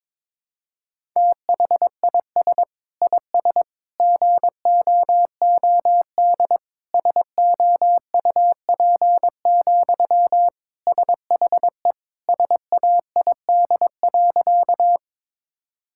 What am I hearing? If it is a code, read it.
Morse code, heard as a keyed tone, about 22 wpm, 700 Hz. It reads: THIS IS GOOD SOUP, SHE SAID.
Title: Sentences from Top 1000 Words - Review Part 6 - 22wpm